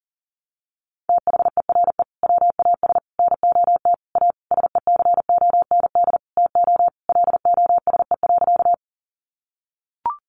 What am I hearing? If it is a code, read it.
Morse code, heard as a keyed tone, about 40 wpm, 700 Hz. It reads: THERE WAS NOT A SECOND TO LOSE.
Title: Sentences from Top 1100 Words 40wpm